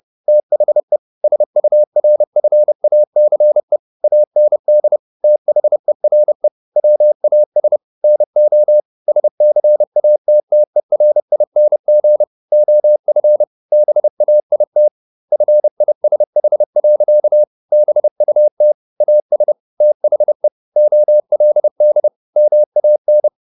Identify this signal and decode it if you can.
Morse code, heard as a keyed tone, about 30 wpm, 600 Hz. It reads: THE SURFACE AND THERE WAS NO SCATTERING OF BAIT FISH. BUT AS THE OLD MAN
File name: the old man and the sea / the old man and the sea 1 30wpm